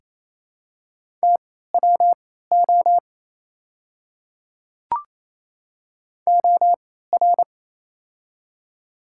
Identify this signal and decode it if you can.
Morse code, heard as a keyed tone, about 28 wpm, spaced as for 16 wpm, 690 Hz. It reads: TWO OR
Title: 3x Character Spacing - Top 100 Words - 28wpm